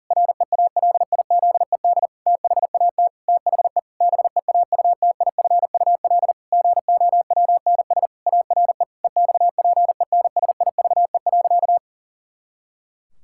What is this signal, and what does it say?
Morse code, heard as a keyed tone, about 40 wpm, 700 Hz. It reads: REALIZED THAT THE BEAUTIFUL GOWNS ARE EXPENSIVE.
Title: Pangram Challenge v2 - 40wpm